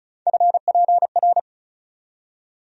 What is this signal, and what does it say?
Morse code, heard as a keyed tone, about 36 wpm, 700 Hz. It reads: FPR
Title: USA Call Sign Postfixes - 3 Letters 35wpm